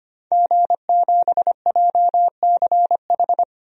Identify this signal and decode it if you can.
Morse code, heard as a keyed tone, about 25 wpm, 700 Hz. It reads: G7JCH
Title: DXCC-100 Call Signs - 5 Characters - 25wpm